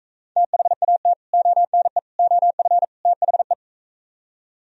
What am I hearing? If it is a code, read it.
Morse code, heard as a keyed tone, about 43 wpm, 690 Hz. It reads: THAT ONE OF THE